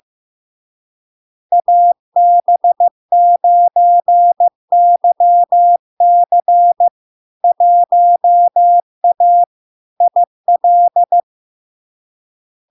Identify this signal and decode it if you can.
Morse code, heard as a keyed tone, about 15 wpm, 700 Hz. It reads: AB9YC 1A IL
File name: LARK-FD-2022-Log-in-CW-15wpm-700-Hz